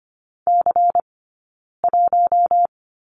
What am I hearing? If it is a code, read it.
Morse code, heard as a keyed tone, about 25 wpm, 700 Hz. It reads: C 1
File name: course / 25:10wpm / 092 Lesson 16d - Letter c - Two Characters 25wpm